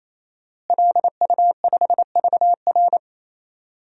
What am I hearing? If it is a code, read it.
Morse code, heard as a keyed tone, about 27 wpm, 700 Hz. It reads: LU5VR
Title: DXCC-100 Call Signs - 5 Characters - Repeated 3x - 28wpm